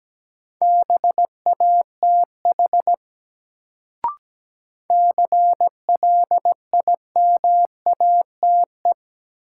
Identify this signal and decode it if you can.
Morse code, heard as a keyed tone, about 17 wpm, 700 Hz. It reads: BATH CLIMATE
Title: Sentences from Top 1600 Words - New Word Drill - 17wpm